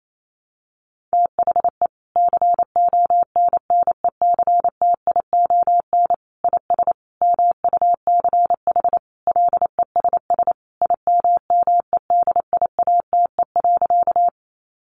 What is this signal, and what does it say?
Morse code, heard as a keyed tone, about 28 wpm, 700 Hz. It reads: THE CONNECTION IS MUCH LESS IMMEDIATE.
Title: Sentences from Top 1300 Words 28wpm